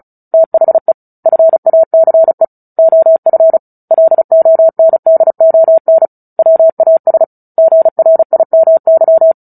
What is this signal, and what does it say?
Morse code, heard as a keyed tone, about 34 wpm, 650 Hz. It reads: THE FACE OF LONDON WAS GRIMY
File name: Book Two / 09-Chapter 9 - 35 WPM